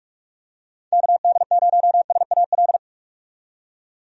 Morse code, heard as a keyed tone, about 45 wpm, 700 Hz. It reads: KD0SAL
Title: USA Call Signs - 6 Characters - 45wpm